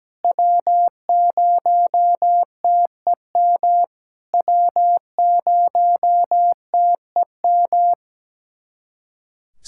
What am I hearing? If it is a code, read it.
Morse code, heard as a keyed tone, about 17 wpm, 700 Hz. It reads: W0TEM W0TEM
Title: SOTA QSO Practice 17wpm